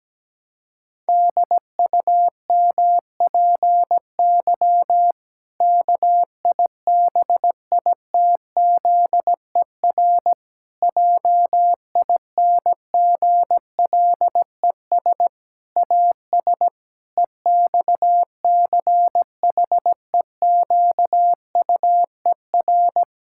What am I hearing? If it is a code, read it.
Morse code, heard as a keyed tone, about 17 wpm, 700 Hz. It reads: DUMPY KIBITZER JINGLES AS EXCHEQUER
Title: Pangram Challenge v2 - 17wpm